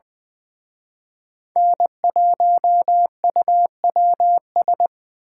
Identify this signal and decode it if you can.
Morse code, heard as a keyed tone, about 24 wpm, 700 Hz. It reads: N1UWS
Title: Speed Racing - USA 1x3 Call Signs 20wpm